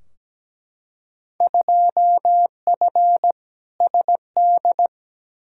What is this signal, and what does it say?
Morse code, heard as a keyed tone, about 17 wpm, 700 Hz. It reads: 2F SD
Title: Field Day Exchanges Encore - 17wpm